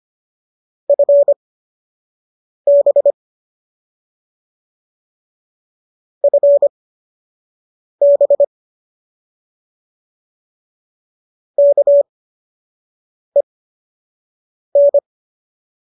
Morse code, heard as a keyed tone, about 25 wpm, spaced as for 6 wpm, 570 Hz. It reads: FB FB KEN
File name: qso003 06